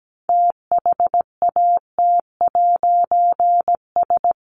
Morse code, heard as a keyed tone, about 17 wpm, 700 Hz. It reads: THAT'S
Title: A Sound Of Thunder 17wpm